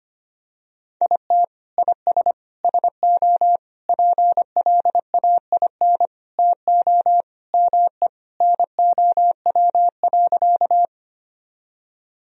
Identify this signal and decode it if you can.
Morse code, heard as a keyed tone, about 25 wpm, 700 Hz. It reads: IT IS SO PLAIN TO ME NOW.
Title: Sentences from Top 1000 Words - Review Part 2 - 25wpm